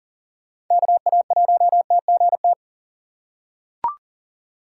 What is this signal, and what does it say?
Morse code, heard as a keyed tone, about 40 wpm, 700 Hz. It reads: KA1TGT
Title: USA Call Signs - 6 Characters - 4x Character Spacing - 40wpm